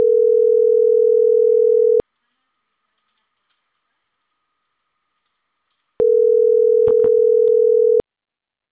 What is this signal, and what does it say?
Ringback tone (North America), 440 + 480 Hz.